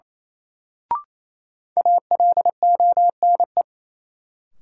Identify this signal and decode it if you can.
Morse code, heard as a keyed tone, about 28 wpm, 700 Hz. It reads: ALONE
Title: Sentences from Top 1000 Words - New Word Drill - 28wpm